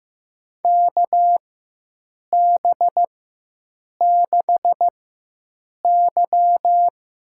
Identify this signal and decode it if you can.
Morse code, heard as a keyed tone, about 15 wpm, spaced as for 7 wpm, 700 Hz. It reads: KB6Y
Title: USA Call Signs - 6 Characters - 4x Character Spacing - 15wpm